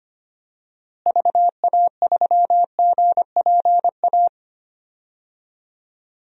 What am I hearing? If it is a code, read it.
Morse code, heard as a keyed tone, about 25 wpm, 700 Hz. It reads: VA3GPA
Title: DXCC-100 Call Signs - 6 Characters - Repeated 2x - 25wpm